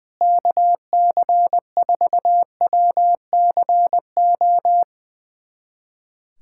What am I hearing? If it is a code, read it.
Morse code, heard as a keyed tone, about 20 wpm, 700 Hz. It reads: KC4WCO
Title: USA Callsign DE Callsign - 6 Characters - 20wpm